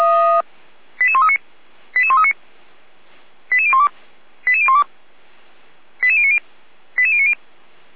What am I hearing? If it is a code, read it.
Five-tone selective call (ZVEI), heard as quick runs of five tones, tones of about 70 ms. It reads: 88139 88139 88012 88012 88090 88090